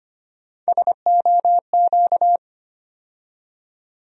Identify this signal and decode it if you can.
Morse code, heard as a keyed tone, about 25 wpm, 700 Hz. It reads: SOQ
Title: USA Call Sign Postfixes - 3 Letters 25wpm